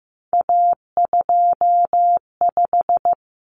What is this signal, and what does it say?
Morse code, heard as a keyed tone, about 15 wpm, 700 Hz. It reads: A25
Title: Common Cut Numbers - 3-Digit 15wpm